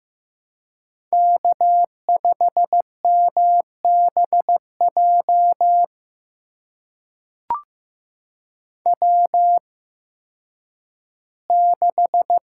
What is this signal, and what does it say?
Morse code, heard as a keyed tone, about 15 wpm, spaced as for 10 wpm, 700 Hz. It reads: K5MBJ W6
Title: USA Call Signs - 5 Characters - 8x Character Spacing - 15wpm